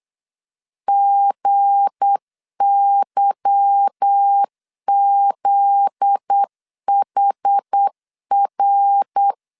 Morse code, heard as a keyed tone, about 8 wpm, 800 Hz. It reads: GYZHR
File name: LEZIONI / 8 YJ45 / 8 YJ45 8wpm 3214